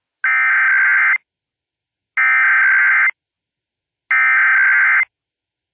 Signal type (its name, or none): Emergency Alert System (SAME) tones